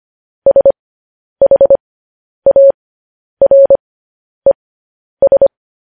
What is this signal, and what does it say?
Morse code, heard as a keyed tone, about 25 wpm, spaced as for 10 wpm, 550 Hz. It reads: SHARES
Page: News Headlines in Morse Code at 10 WPM